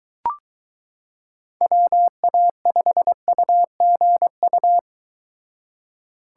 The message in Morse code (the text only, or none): WA5UGU